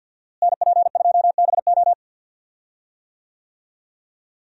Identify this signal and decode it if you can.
Morse code, heard as a keyed tone, about 46 wpm, 700 Hz. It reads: NP2BO
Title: USA Call Signs - 5 Characters - Repeated 2x - 50wpm